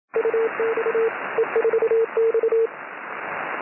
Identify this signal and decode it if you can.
Morse code, heard as a keyed tone, about 27 wpm, 450 Hz, with radio noise. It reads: UX E4X
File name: E4X CW